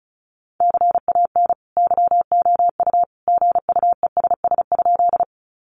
Morse code, heard as a keyed tone, about 34 wpm, 700 Hz. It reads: CAN YOU GUESS?
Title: Sentences from Top 1500 Words 35wpm